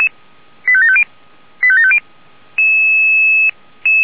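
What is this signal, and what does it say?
Five-tone selective call (ZVEI), heard as quick runs of five tones, tones of about 70 ms. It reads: 85750 85750
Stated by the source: ZVEI five-tone call received by SDR.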